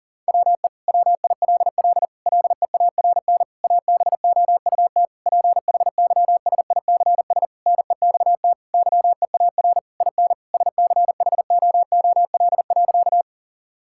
Morse code, heard as a keyed tone, about 40 wpm, 700 Hz. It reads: WE WILL LEARN ABOUT PHYSICS NEXT YEAR IN SCHOOL.
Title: Sentences from Top 1000 Words - Review Part 5 - 40wpm